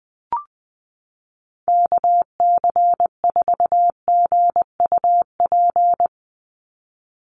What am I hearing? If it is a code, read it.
Morse code, heard as a keyed tone, about 20 wpm, 700 Hz. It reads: KC4GUP
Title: USA Call Signs 20wpm